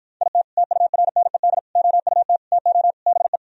Morse code, heard as a keyed tone, about 56 wpm, 700 Hz. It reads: IT TURNED OUT TO BE